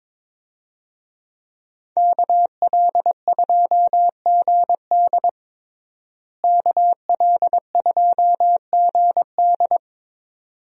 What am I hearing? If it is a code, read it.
Morse code, heard as a keyed tone, about 22 wpm, 700 Hz. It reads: KL2GD KL2GD
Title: USA Call Signs - 5 Characters - Repeated 2x - 22wpm